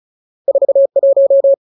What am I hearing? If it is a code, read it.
Morse code, heard as a keyed tone, about 34 wpm, 550 Hz. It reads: 41
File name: Ultimate Word List / 35WPM Ultimate